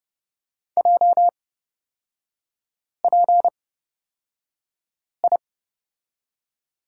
Morse code, heard as a keyed tone, about 31 wpm, spaced as for 5 wpm, 700 Hz. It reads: JPI
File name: Lesson 38e - Letter q - Three Characters 30wpm